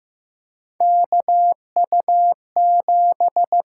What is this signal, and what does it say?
Morse code, heard as a keyed tone, about 15 wpm, 700 Hz. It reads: KU7
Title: USA Call Signs - 4 Characters - 6x Character Spacing - 15wpm